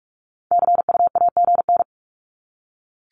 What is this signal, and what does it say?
Morse code, heard as a keyed tone, about 44 wpm, 700 Hz. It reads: CUAGN
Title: Abbreviations 45wpm